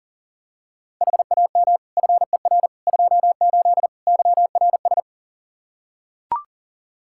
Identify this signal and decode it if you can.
Morse code, heard as a keyed tone, about 40 wpm, 700 Hz. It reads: HAM FER 28 YRS
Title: Practice QSO - Exchange 3 - More Info - 40wpm